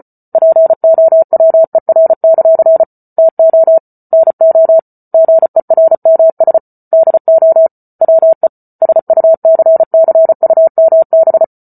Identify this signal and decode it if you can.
Morse code, heard as a keyed tone, about 34 wpm, 650 Hz. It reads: POWER; TO NO GERMS DO WE SUCCUMB
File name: Book Two / 08-Chapter 8 - 34 WPM